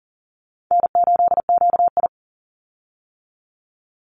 Morse code, heard as a keyed tone, about 40 wpm, 700 Hz. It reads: N8QI
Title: USA Call Signs 40wpm